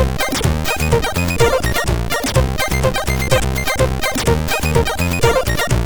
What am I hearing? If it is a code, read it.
There is no code.